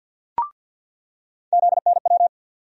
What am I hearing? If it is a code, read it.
Morse code, heard as a keyed tone, about 47 wpm, 700 Hz. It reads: ZNW
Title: USA Call Sign Postfixes - 3 Letters 50wpm